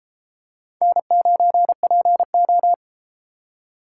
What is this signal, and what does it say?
Morse code, heard as a keyed tone, about 32 wpm, 700 Hz. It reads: N9PO